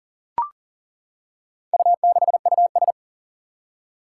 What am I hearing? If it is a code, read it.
Morse code, heard as a keyed tone, about 40 wpm, 700 Hz. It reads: U6US
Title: Four Challenging Letter-Number Combinations 40wpm